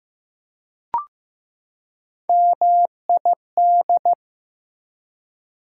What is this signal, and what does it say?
Morse code, heard as a keyed tone, about 15 wpm, 700 Hz. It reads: MID
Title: Prefixes and Suffixes 15wpm